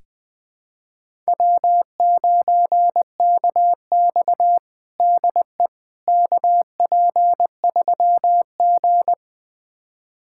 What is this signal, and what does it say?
Morse code, heard as a keyed tone, about 21 wpm, 700 Hz. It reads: W9KX DE KP3G
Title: USA Callsign DE Callsign - 4 Characters - 20wpm